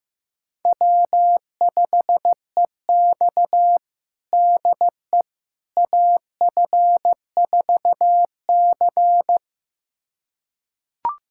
Morse code, heard as a keyed tone, about 15 wpm, 700 Hz. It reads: W5EX DE AF4C
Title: USA Callsign DE Callsign - 4 Characters - 15wpm